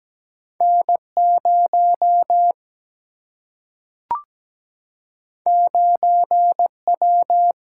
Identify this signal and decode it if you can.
Morse code, heard as a keyed tone, about 17 wpm, 700 Hz. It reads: N 0 9 W
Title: Single-Letter Single-Number 17wpm